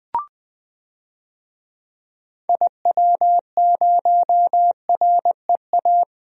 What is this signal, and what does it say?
Morse code, heard as a keyed tone, about 20 wpm, 700 Hz. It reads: IW0REA
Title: DXCC-100 Call Signs - 6 Characters - Repeated 2x - 20wpm